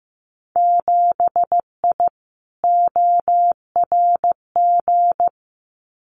Morse code, heard as a keyed tone, about 15 wpm, 700 Hz. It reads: 7I ORG
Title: Winter Field Day Exchanges Encore 15wpm